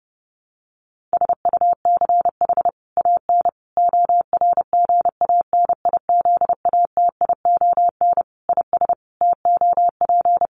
Morse code, heard as a keyed tone, about 30 wpm, 700 Hz. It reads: SUCH AN ORGANIZATION IS TOP
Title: Sentences from Top 1100 Words 30wpm